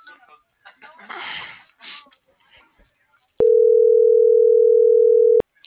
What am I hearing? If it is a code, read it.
Ringback tone (North America), 440 + 480 Hz.